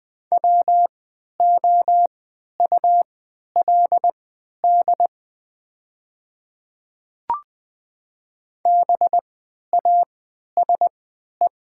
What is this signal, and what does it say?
Morse code, heard as a keyed tone, about 20 wpm, spaced as for 11 wpm, 700 Hz. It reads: WOULD BASE
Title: 3x Character Spacing - Top 300 Words - 20wpm